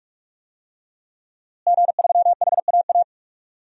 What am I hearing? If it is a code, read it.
Morse code, heard as a keyed tone, about 46 wpm, 700 Hz. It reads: G3HAA
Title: DXCC-100 Call Signs - 5 Characters - Repeated 2x - 45wpm